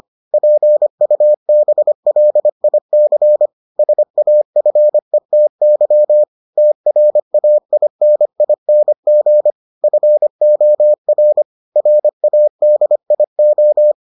Morse code, heard as a keyed tone, about 25 wpm, 600 Hz. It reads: PUBLIC SAFETY TRAINING FOR RADIO